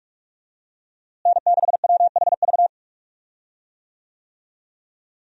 Morse code, heard as a keyed tone, about 46 wpm, 700 Hz. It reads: N6WHV
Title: USA Call Signs - 5 Characters - Repeated 2x - 45wpm